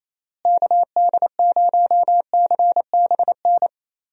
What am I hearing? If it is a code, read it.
Morse code, heard as a keyed tone, about 28 wpm, 700 Hz. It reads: KD0CBN
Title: USA Call Signs - 6 Characters - Repeated 3x - 28wpm